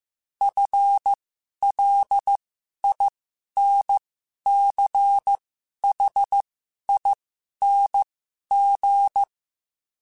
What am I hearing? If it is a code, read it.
Morse code, heard as a keyed tone, about 15 wpm, spaced as for 11 wpm, 800 Hz. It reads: FLINCHING
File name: Special INGs-mp3s / Special INGs-mp3s / Sngl Ing @10